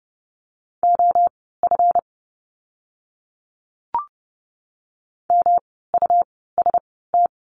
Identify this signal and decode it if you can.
Morse code, heard as a keyed tone, about 30 wpm, spaced as for 17 wpm, 700 Hz. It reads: OF MUST
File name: course / 30:10wpm / 237 Lesson 38f - Letter q - Words 30wpm